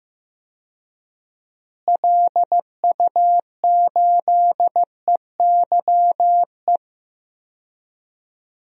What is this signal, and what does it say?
Morse code, heard as a keyed tone, about 15 wpm, 700 Hz. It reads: LU8EYE
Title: DXCC-100 Call Signs - 6 Characters - Repeated 2x - 15wpm